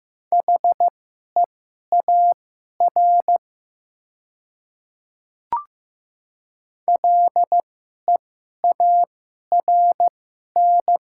Morse code, heard as a keyed tone, about 15 wpm, spaced as for 11 wpm, 700 Hz. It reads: HEAR LEARN
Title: 2x Character Spacing - Top 300 Words - 15wpm